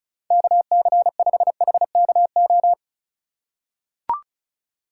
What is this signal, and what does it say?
Morse code, heard as a keyed tone, about 36 wpm, 700 Hz. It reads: KC5HKO
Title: USA Call Signs - 2x Character Spacing - 35wpm